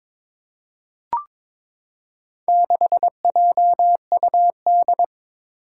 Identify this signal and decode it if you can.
Morse code, heard as a keyed tone, about 22 wpm, 700 Hz. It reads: 6JUD
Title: Four Challenging Letter-Number Combinations 22wpm